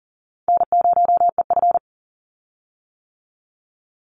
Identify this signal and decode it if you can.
Morse code, heard as a keyed tone, about 40 wpm, 700 Hz. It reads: N0EF
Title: Call Signs - 40wpm